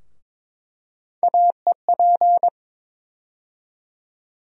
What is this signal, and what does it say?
Morse code, heard as a keyed tone, about 22 wpm, 700 Hz. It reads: AEP
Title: USA Call Sign Postfixes - 3 Letters 22wpm